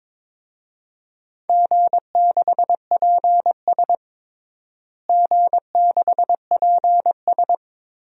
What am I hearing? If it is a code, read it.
Morse code, heard as a keyed tone, about 22 wpm, 700 Hz. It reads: G6PS G6PS
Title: DXCC-100 Call Signs - 4 Characters - Repeated 3x - 22wpm